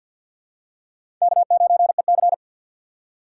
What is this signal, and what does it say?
Morse code, heard as a keyed tone, about 47 wpm, 690 Hz. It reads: K9EC
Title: USA Call Signs - 4 Characters - 6x Character Spacing - 50wpm